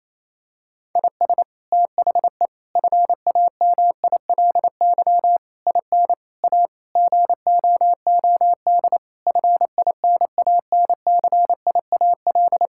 Morse code, heard as a keyed tone, about 28 wpm, 700 Hz. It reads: IS THE FAMILY IN A GOOD FINANCIAL
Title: Sentences from Top 1000 Words - Review Part 5 - 28wpm